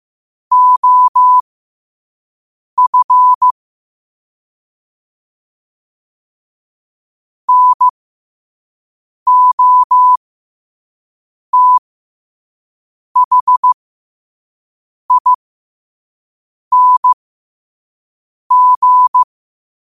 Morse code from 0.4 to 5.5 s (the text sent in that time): OF